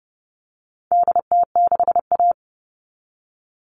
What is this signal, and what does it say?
Morse code, heard as a keyed tone, about 30 wpm, 700 Hz. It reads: DT6A
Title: All Cut Numbers - 4-Digit 30wpm